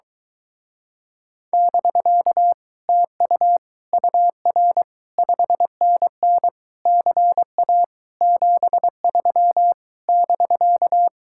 Morse code, heard as a keyed tone, about 23 wpm, 700 Hz. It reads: <BK> TU UR 5NN CA 73 <BK>